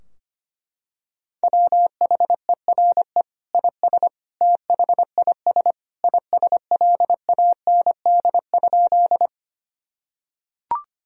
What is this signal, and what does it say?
Morse code, heard as a keyed tone, about 25 wpm, 700 Hz. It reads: WHERE IS THIS ISLAND?
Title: Sentences from Top 1000 Words - Review Part 3 - 25wpm